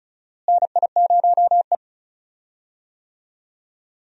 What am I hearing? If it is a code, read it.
Morse code, heard as a keyed tone, about 34 wpm, 700 Hz. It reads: NI0E
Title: USA Call Signs - 4 Characters - Repeated 2x - 35wpm